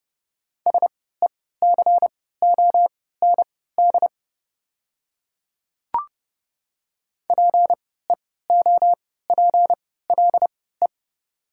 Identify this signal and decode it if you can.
Morse code, heard as a keyed tone, about 30 wpm, spaced as for 17 wpm, 690 Hz. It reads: SECOND PEOPLE
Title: 3x Character Spacing - Top 300 Words - 30wpm